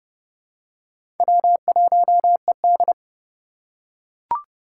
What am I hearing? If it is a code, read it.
Morse code, heard as a keyed tone, about 30 wpm, 700 Hz. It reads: W1ED
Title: USA Call Signs - 4 Characters - 4x Character Spacing - 30wpm